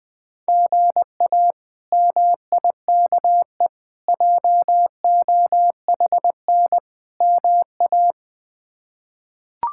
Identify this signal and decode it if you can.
Morse code, heard as a keyed tone, about 20 wpm, 700 Hz. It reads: GA MIKE JOHN MA
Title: K1USN SST QSO Practice - 20wpm